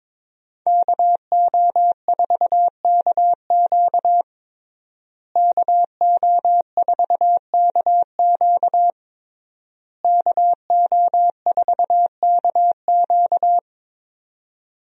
Morse code, heard as a keyed tone, about 22 wpm, 700 Hz. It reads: KO4KQ KO4KQ KO4KQ
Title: USA Call Signs - 5 Characters - Repeated 3x - 22wpm